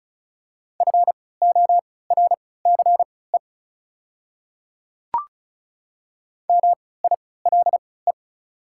Morse code, heard as a keyed tone, about 36 wpm, spaced as for 20 wpm, 690 Hz. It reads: FORCE MILE
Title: 3x Character Spacing - Top 500 Words - 35wpm